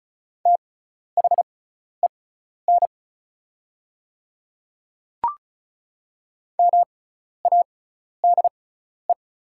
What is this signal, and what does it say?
Morse code, heard as a keyed tone, about 36 wpm, spaced as for 12 wpm, 700 Hz. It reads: THEN MADE
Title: 6x Character Spacing - Top 200 Words - 35wpm